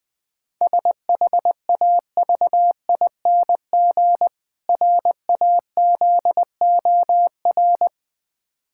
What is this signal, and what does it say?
Morse code, heard as a keyed tone, about 20 wpm, 700 Hz. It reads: SHAVING RAZOR
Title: Difficult Words - Speed-Racing with Hints - 20wpm